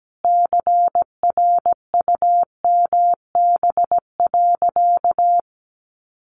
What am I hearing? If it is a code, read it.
Morse code, heard as a keyed tone, about 17 wpm, 700 Hz. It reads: CRUMB.
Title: Additional Easy Phrases 17wpm